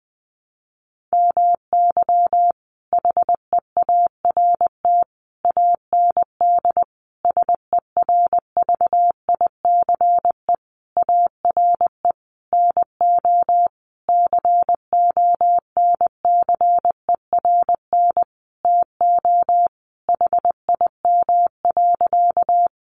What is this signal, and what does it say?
Morse code, heard as a keyed tone, about 20 wpm, 700 Hz. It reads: MY HEART AND SERVICE ARE NO CONCERN TO HIM.
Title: Sentences from Top 1200 Words 20wpm